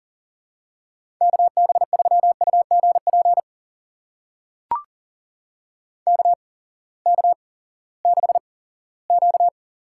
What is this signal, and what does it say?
Morse code, heard as a keyed tone, about 40 wpm, spaced as for 20 wpm, 700 Hz. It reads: KB3UGP KK6Q
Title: USA Call Signs - 6 Characters - 8x Character Spacing - 40wpm